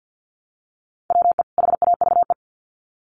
Morse code, heard as a keyed tone, about 50 wpm, 700 Hz. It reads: WE HAVE
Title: Sets of 2 Words 50wpm